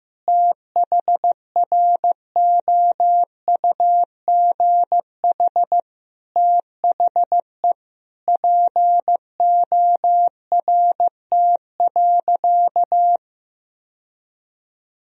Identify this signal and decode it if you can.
Morse code, heard as a keyed tone, about 15 wpm, 700 Hz. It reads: THROUGH THE PORT.
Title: Sentences from Top 1000 Words - Review Part 1 - 15wpm